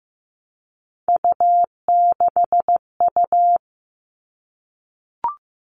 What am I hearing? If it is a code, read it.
Morse code, heard as a keyed tone, about 15 wpm, 700 Hz. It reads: U6U
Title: All Cut Numbers - 3-Digit 15wpm